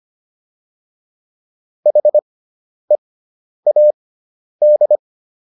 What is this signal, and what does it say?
Morse code, heard as a keyed tone, about 25 wpm, spaced as for 10 wpm, 600 Hz. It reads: HEAD